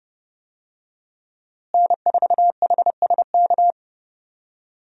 Morse code, heard as a keyed tone, about 30 wpm, 700 Hz. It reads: N4HSK
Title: USA Call Signs - 5 Characters - Repeated 3x - 30wpm